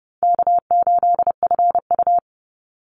Morse code, heard as a keyed tone, about 30 wpm, 700 Hz. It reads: K8FU
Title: Easy Call Signs Repeated 3x - 30wpm